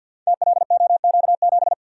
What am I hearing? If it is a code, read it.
Morse code, heard as a keyed tone, about 51 wpm, 700 Hz. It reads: TLOQ7
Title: Code Groups - 5 Letter:Numbers - 50wpm